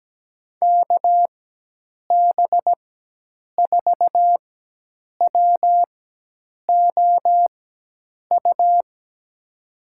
Morse code, heard as a keyed tone, about 17 wpm, spaced as for 8 wpm, 700 Hz. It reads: KB4WOU
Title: USA Call Signs - 4x Character Spacing - 17wpm